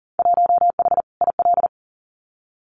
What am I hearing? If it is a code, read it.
Morse code, heard as a keyed tone, about 40 wpm, 700 Hz. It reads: 1H IL
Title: Winter Field Day Exchanges 40wpm